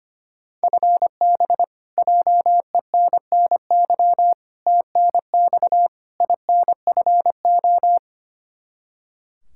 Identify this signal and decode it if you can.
Morse code, heard as a keyed tone, about 25 wpm, 700 Hz. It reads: FB JENNY TNX INFO
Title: K1USN SST QSO Practice 25wpm